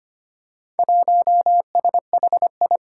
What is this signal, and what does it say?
Morse code, heard as a keyed tone, about 25 wpm, 700 Hz. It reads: 1SHI
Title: Four Challenging Letter-Number Combinations 25wpm